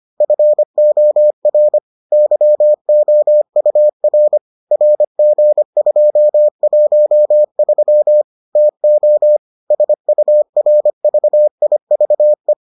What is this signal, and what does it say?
Morse code, heard as a keyed tone, about 25 wpm, 600 Hz. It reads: FOR YOUR RG213 TO SURVIVE